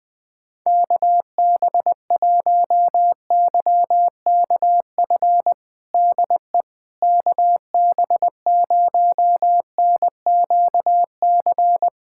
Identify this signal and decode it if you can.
Morse code, heard as a keyed tone, about 20 wpm, 700 Hz. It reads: KB1YKF DE KB0NQC
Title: USA Callsign DE Callsign - 6 Characters - 20wpm